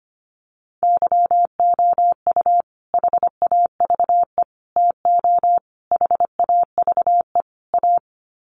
Morse code, heard as a keyed tone, about 25 wpm, 700 Hz. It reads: YOU HAVE TO HAVE A
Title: Sets of 5 Words Encore 25wpm